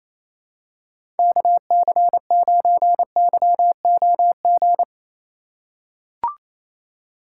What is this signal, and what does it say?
Morse code, heard as a keyed tone, about 28 wpm, 700 Hz. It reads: KC9YOG
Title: USA Call Signs - 8x Character Spacing - 28wpm